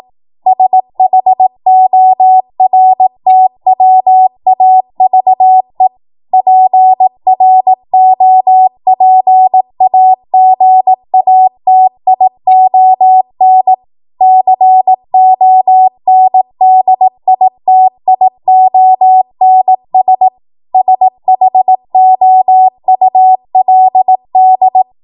Morse code, heard as a keyed tone, about 18 wpm, 750 Hz. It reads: SHORTWAVE PROPAGATION CONDITIONS SHOULD
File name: Morse / 18WPM